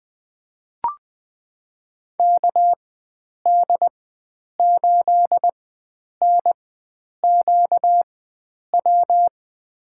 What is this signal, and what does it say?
Morse code, heard as a keyed tone, about 20 wpm, spaced as for 9 wpm, 700 Hz. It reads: KD8NQW
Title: USA Call Signs - 6 Characters - 4x Character Spacing - 20wpm